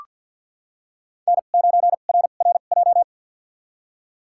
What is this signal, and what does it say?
Morse code, heard as a keyed tone, about 47 wpm, spaced as for 35 wpm, 700 Hz. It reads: N9RRJ
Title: USA Call Signs - 5 Characters - 2x Character Spacing - 50wpm